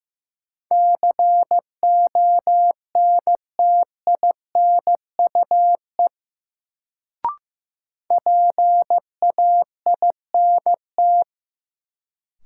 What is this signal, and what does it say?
Morse code, heard as a keyed tone, about 15 wpm, 700 Hz. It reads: CONTINUE PAINT
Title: Sentences from Top 1000 Words - Review Part 6 - New Word Drill - 15wpm